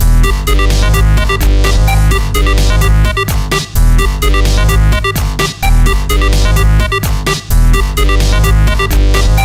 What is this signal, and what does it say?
There is no code.